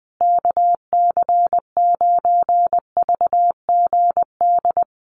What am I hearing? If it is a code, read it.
Morse code, heard as a keyed tone, about 20 wpm, 700 Hz. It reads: KC9VGD
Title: USA Call Signs Repeated 3x 20wpm